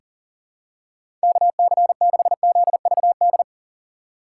Morse code, heard as a keyed tone, about 40 wpm, 700 Hz. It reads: KC6ZVD
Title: USA Call Signs - 6 Characters - 6x Character Spacing - 40wpm